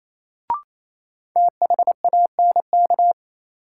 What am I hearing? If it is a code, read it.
Morse code, heard as a keyed tone, about 28 wpm, 700 Hz. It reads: THANK
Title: Sets of 2 Words - New Word Drill - 28wpm